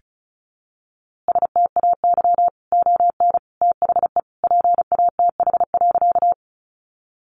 Morse code, heard as a keyed tone, about 34 wpm, 700 Hz. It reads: STAY ON THE PATH.
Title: A Sound Of Thunder 35wpm